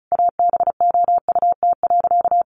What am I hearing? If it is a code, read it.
Morse code, heard as a keyed tone, about 34 wpm, 700 Hz. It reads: ABOUT.
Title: Sentences from Top 1300 Words 35wpm